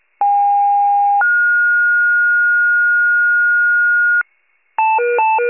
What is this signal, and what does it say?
Two-tone page, heard as a one-second tone, then a three-second tone, about 810 then 1430 Hz, with noise. It is heard from 0.2 to 4.2 s.